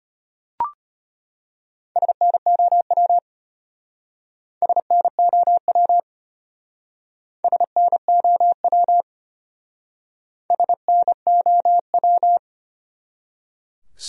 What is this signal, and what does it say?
Morse code, heard as a keyed tone, about 35 wpm, 700 Hz. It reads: SNOW SNOW SNOW SNOW